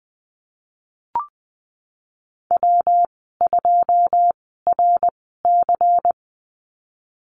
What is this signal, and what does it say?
Morse code, heard as a keyed tone, about 20 wpm, spaced as for 15 wpm, 700 Hz. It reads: W2RC
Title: 122 Lesson 20g - Number 3 - Call Signs 20wpm